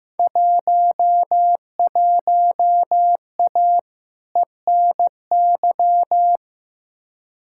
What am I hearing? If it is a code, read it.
Morse code, heard as a keyed tone, about 15 wpm, 700 Hz. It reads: 11A ENY
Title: Field Day Exchanges Encore - 15wpm